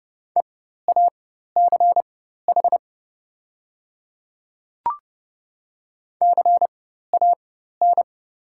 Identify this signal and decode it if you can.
Morse code, heard as a keyed tone, about 30 wpm, spaced as for 14 wpm, 690 Hz. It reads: EACH CAN
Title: 4x Character Spacing - Top 100 Words - 30wpm